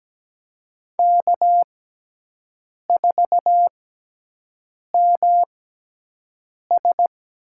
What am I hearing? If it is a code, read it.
Morse code, heard as a keyed tone, about 17 wpm, spaced as for 6 wpm, 700 Hz. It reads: K4MS
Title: USA Call Signs - 5 Characters - 6x Character Spacing - 17wpm